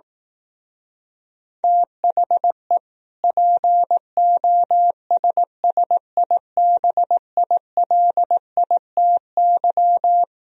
Morse code, heard as a keyed tone, about 18 wpm, 700 Hz. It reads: THE POSSIBILITY
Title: Speed-Racing - Sets of 3 Words 15wpm